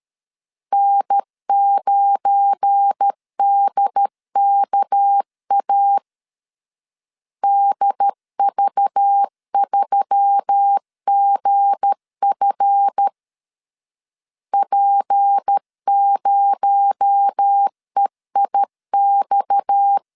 Morse code, heard as a keyed tone, about 12 wpm, 800 Hz. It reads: N9DKA DV3GF P0EIX
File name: LEZIONI / 7 GF9Z / 7 GF9 12wpm 2319